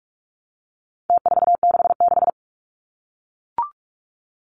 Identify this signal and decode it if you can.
Morse code, heard as a keyed tone, about 44 wpm, 700 Hz. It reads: T466
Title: Numbers - Contest Style - 4-Digit 45wpm